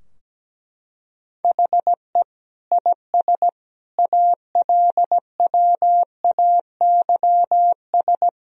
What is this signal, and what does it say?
Morse code, heard as a keyed tone, about 17 wpm, 700 Hz. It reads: HE IS ALWAYS
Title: Sentences from Top 1000 Words - Review Part 3 - 17wpm